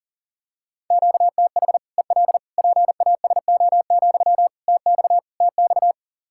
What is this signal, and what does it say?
Morse code, heard as a keyed tone, about 40 wpm, 700 Hz. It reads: QTH EL PASO, TX TX
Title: Practice QSO - Exchange 1 - Beginning - 40wpm